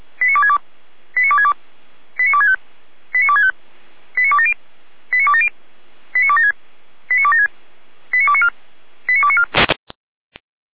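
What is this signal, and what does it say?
Five-tone selective call (ZVEI), heard as quick runs of five tones, tones of about 70 ms. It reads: 89262 89262 89265 89265 89270 89270 89276 89276 89294 89294
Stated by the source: Received on an SDR, ZVEI five-tone code 89280.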